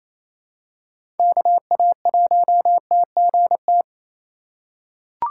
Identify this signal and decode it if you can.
Morse code, heard as a keyed tone, about 28 wpm, 700 Hz. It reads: KA1TGT
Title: USA Call Signs - 8x Character Spacing - 28wpm